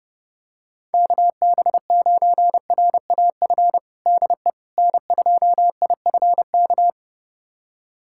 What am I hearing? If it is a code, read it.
Morse code, heard as a keyed tone, about 32 wpm, 700 Hz. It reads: KB9RAF DE N2IFK
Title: USA Callsign DE Callsign 30wpm